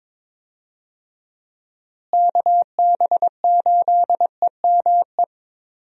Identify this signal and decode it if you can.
Morse code, heard as a keyed tone, about 22 wpm, 700 Hz. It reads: KB8EME
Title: USA Call Signs - 6 Characters - Repeated 3x - 22wpm